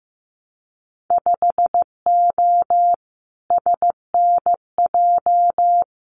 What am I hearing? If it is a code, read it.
Morse code, heard as a keyed tone, about 15 wpm, 700 Hz. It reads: 5O SNJ
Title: Winter Field Day Exchanges Encore 15wpm